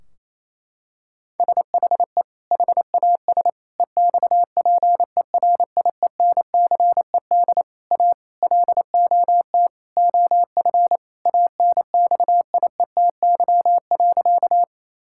Morse code, heard as a keyed tone, about 28 wpm, 700 Hz. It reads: SHE HAS EXPERIENCED A LOT OF ANXIETY.
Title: Sentences from Top 1600 Words 28wpm